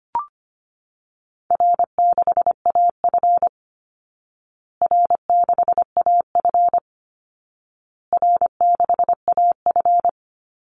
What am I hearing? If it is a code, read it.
Morse code, heard as a keyed tone, about 25 wpm, 700 Hz. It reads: R6AF R6AF R6AF
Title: Easy Call Signs Repeated 3x - 25wpm